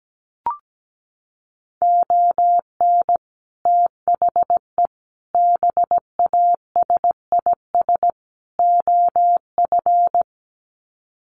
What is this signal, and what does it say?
Morse code, heard as a keyed tone, about 17 wpm, 700 Hz. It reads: ON THE BASIS OF
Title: Sets of 4 Words 17wpm